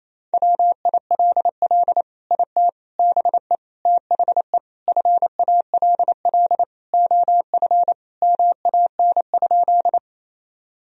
Morse code, heard as a keyed tone, about 28 wpm, 700 Hz. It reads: WILL IT BE THE FALL OF MAN?
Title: Sentences from Top 1000 Words - Review Part 2 - 28wpm